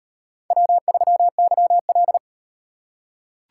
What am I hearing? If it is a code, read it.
Morse code, heard as a keyed tone, about 38 wpm, 700 Hz. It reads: W3YL